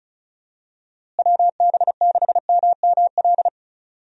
Morse code, heard as a keyed tone, about 34 wpm, 700 Hz. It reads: WB6MML
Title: USA Call Signs - 6 Characters - Repeated 3x - 35wpm